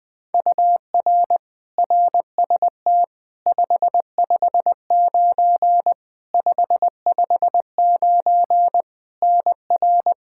Morse code, heard as a keyed tone, about 20 wpm, 700 Hz. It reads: UR RST 559 559 NR
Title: Practice QSO - Exchange 1 - Beginning - 20wpm